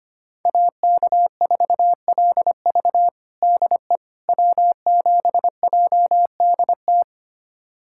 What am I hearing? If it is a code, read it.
Morse code, heard as a keyed tone, about 25 wpm, 700 Hz. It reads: AK4LV DE W7JDT
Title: USA Callsign DE Callsign - 5 Characters - 25wpm